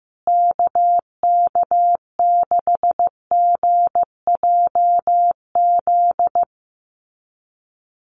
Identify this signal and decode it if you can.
Morse code, heard as a keyed tone, about 15 wpm, 700 Hz. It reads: KK6GJZ
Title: USA Call Signs Repeated 3x 15wpm